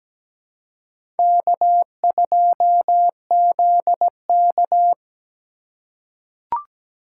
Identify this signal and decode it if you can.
Morse code, heard as a keyed tone, about 17 wpm, 700 Hz. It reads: K2ZK
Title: USA Call Signs - 4 Characters - 2x Character Spacing - 17wpm